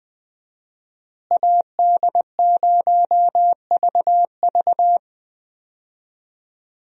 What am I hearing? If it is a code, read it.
Morse code, heard as a keyed tone, about 20 wpm, 700 Hz. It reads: AD0VV
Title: USA Call Signs - 5 Characters - Repeated 2x - 20wpm